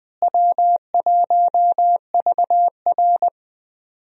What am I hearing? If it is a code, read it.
Morse code, heard as a keyed tone, about 20 wpm, 700 Hz. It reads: W1VR
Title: USA Call Signs - 4 Characters - Repeated 2x - 20wpm